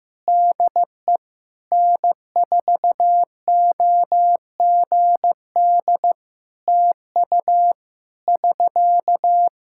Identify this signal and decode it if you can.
Morse code, heard as a keyed tone, about 16 wpm, 700 Hz. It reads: DE N4OGD TU <SK>
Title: Practice QSO - Exchange 4 - Ending - 15wpm